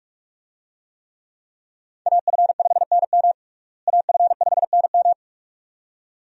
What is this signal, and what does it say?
Morse code, heard as a keyed tone, about 46 wpm, 700 Hz. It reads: AF5NM AF5NM
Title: USA Call Signs - 5 Characters - Repeated 2x - 45wpm